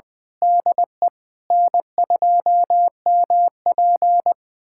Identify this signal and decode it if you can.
Morse code, heard as a keyed tone, about 21 wpm, 690 Hz. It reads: DE N2MP
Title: USA Callsign DE Callsign 20wpm